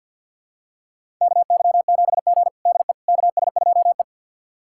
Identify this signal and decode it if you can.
Morse code, heard as a keyed tone, about 50 wpm, 700 Hz. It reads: KY7G DE KS2E
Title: USA Callsign DE Callsign - 4 Characters - 50wpm